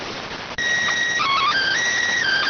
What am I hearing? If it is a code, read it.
SSTV (slow-scan television), PD 290.